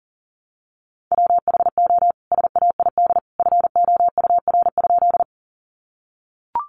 Morse code, heard as a keyed tone, about 40 wpm, 700 Hz. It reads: WHO SAID FOUR?
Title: Sentences from Top 300 Words 40wpm